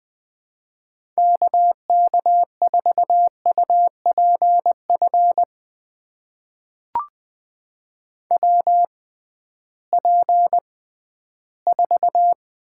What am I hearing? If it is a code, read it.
Morse code, heard as a keyed tone, about 20 wpm, spaced as for 13 wpm, 700 Hz. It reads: KK4UPF WP4
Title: USA Call Signs - 6x Character Spacing - 20wpm